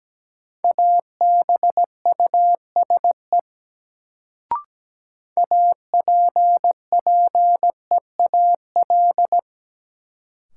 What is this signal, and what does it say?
Morse code, heard as a keyed tone, about 17 wpm, 700 Hz. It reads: ABUSE APPEAL
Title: Sentences from Top 1300 Words - New Word Drill - 17wpm